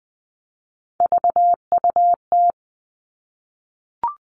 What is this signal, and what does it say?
Morse code, heard as a keyed tone, about 20 wpm, 700 Hz. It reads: VUT
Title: All Cut Numbers - 3-Digit 20wpm